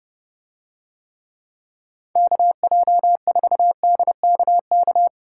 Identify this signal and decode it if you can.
Morse code, heard as a keyed tone, about 29 wpm, 700 Hz. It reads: KJ4DKK